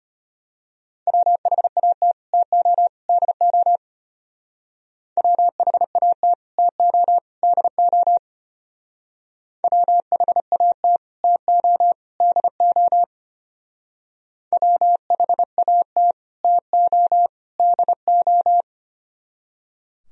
Morse code, heard as a keyed tone, about 32 wpm, 700 Hz. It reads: WHAT TO DO WHAT TO DO WHAT TO DO WHAT TO DO